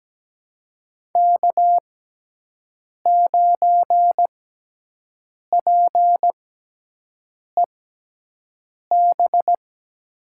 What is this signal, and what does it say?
Morse code, heard as a keyed tone, about 17 wpm, spaced as for 6 wpm, 700 Hz. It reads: K9PEB
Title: USA Call Signs - 5 Characters - 6x Character Spacing - 17wpm